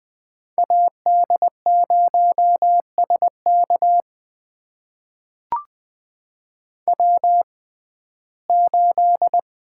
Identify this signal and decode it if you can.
Morse code, heard as a keyed tone, about 21 wpm, spaced as for 15 wpm, 700 Hz. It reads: AD0SK W8
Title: USA Call Signs - 5 Characters - 6x Character Spacing - 20wpm